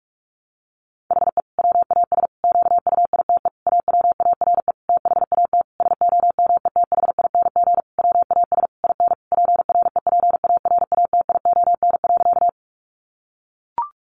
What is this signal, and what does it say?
Morse code, heard as a keyed tone, about 44 wpm, 700 Hz. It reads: HE WAS QUITE AWARE THAT SOMETHING WAS IN PREPARATION.
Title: Sentences from Top 1400 Words 45wpm